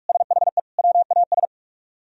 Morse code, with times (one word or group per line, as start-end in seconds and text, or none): 0.1-0.6 SHE
0.8-1.4 WAS